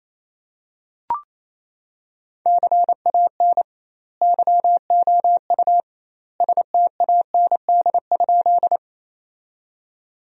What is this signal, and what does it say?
Morse code, heard as a keyed tone, about 27 wpm, 700 Hz. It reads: CAN YOU STAND?
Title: 2x Word-Spacing - Sentences from Top 300 Words - 28wpm